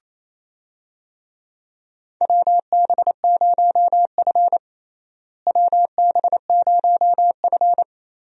Morse code, heard as a keyed tone, about 28 wpm, 700 Hz. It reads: WB0F WB0F
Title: USA Call Signs - 4 Characters - Repeated 2x - 28wpm